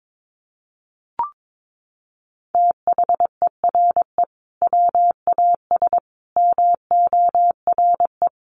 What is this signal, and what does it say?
Morse code, heard as a keyed tone, about 22 wpm, 700 Hz. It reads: THERE WAS MORE
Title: Sentences from Top 500 Words 22wpm